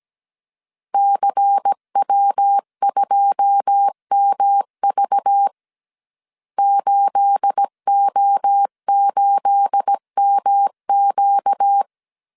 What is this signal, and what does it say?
Morse code, heard as a keyed tone, about 17 wpm, 800 Hz. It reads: CW2MV 8O8MQ
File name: CW / DET / 1 9 1500C S A 16wpm 2828